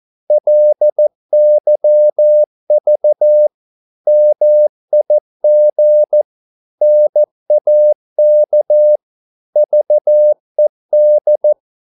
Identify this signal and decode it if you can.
Morse code, heard as a keyed tone, about 14 wpm, 600 Hz. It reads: LYV MIG NAK VED
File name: Korte ord DK 14wpm